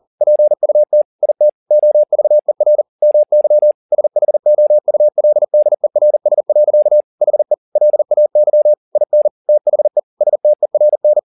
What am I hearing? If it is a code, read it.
Morse code, heard as a keyed tone, about 40 wpm, 600 Hz. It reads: PUT IT OVER MY SHOULDERS. HE LAY IN THE STERN
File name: the old man and the sea 2 40wpm